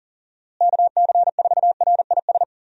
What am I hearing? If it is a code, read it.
Morse code, heard as a keyed tone, about 38 wpm, 700 Hz. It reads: KC4RIS
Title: USA Call Signs - 6 Characters - Repeated 3x - 40wpm